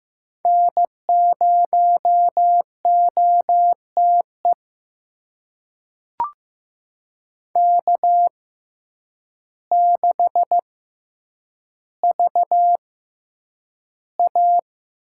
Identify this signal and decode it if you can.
Morse code, heard as a keyed tone, about 15 wpm, spaced as for 8 wpm, 700 Hz. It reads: N0OTE K6VA
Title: USA Call Signs - 5 Characters - 6x Character Spacing - 15wpm